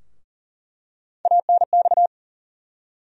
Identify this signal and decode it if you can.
Morse code, heard as a keyed tone, about 38 wpm, 700 Hz. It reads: ANX
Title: USA Call Sign Postfixes - 3 Letters 40wpm